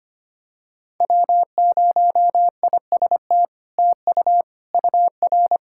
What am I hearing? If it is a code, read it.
Morse code, heard as a keyed tone, about 25 wpm, 700 Hz. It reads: W0IST TU UR
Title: SOTA QSO Practice 25wpm